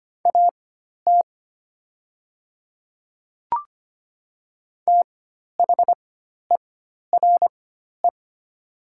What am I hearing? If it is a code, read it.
Morse code, heard as a keyed tone, about 25 wpm, spaced as for 12 wpm, 700 Hz. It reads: AT THERE
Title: 4x Character Spacing - Top 100 Words - 25wpm